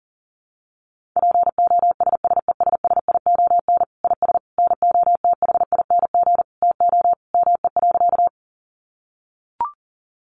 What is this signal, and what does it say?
Morse code, heard as a keyed tone, about 40 wpm, 700 Hz. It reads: POSSESSION IS NOTHING TO ME.
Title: Sentences from Top 1400 Words 40wpm